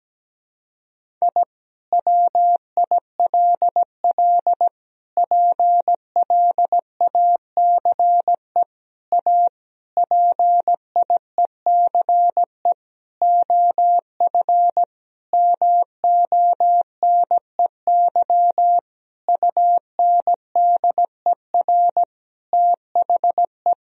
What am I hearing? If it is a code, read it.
Morse code, heard as a keyed tone, about 17 wpm, 700 Hz. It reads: I WILL PLACE A PIECE OF MONEY UNDER THE
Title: Sentences from Top 1000 Words - Review Part 2 - 17wpm